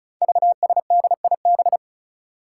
Morse code, heard as a keyed tone, about 36 wpm, 700 Hz. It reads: VSDIB
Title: Five Challenging Letter-Number Combinations 35wpm